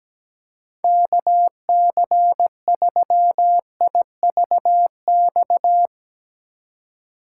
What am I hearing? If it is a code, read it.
Morse code, heard as a keyed tone, about 17 wpm, 700 Hz. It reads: KC3IVX
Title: USA Call Signs - 8x Character Spacing - 17wpm